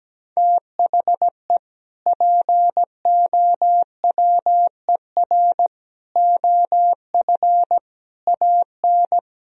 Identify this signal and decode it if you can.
Morse code, heard as a keyed tone, about 17 wpm, 700 Hz. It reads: THE POWER OF AN
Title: Sentences from Top 1000 Words - Review Part 4 - 17wpm